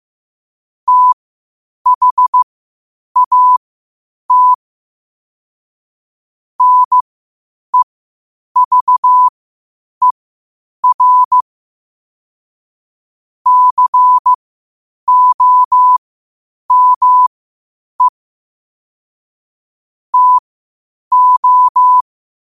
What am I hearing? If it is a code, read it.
Morse code, heard as a keyed tone, about 15 wpm, spaced as for 8 wpm, 1000 Hz. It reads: THAT NEVER COME TO